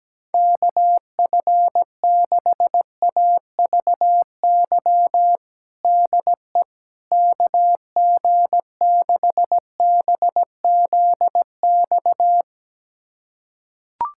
Morse code, heard as a keyed tone, about 17 wpm, 700 Hz. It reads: KF6AVY DE KG6BZX
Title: USA Callsign DE Callsign - 6 Characters - 17wpm